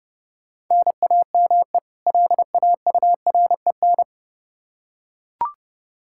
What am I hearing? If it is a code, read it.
Morse code, heard as a keyed tone, about 30 wpm, 700 Hz. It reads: NAME LAUREN
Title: K1USN SST QSO Practice 30wpm